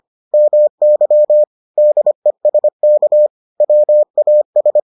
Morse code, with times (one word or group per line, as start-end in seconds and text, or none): 0.3-1.4 MY
1.8-3.3 DESK
3.6-4.8 WAS